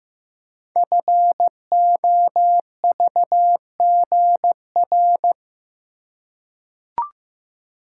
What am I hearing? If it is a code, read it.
Morse code, heard as a keyed tone, about 15 wpm, 700 Hz. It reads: FOVGR
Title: Code Groups - 5 Letters - 15wpm